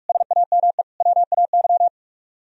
Morse code, heard as a keyed tone, about 58 wpm, 700 Hz. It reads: SAME WAY